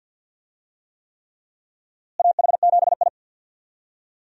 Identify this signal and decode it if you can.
Morse code, heard as a keyed tone, about 47 wpm, 700 Hz. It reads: AH7I
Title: USA Call Signs - 4 Characters - Repeated 3x - 50wpm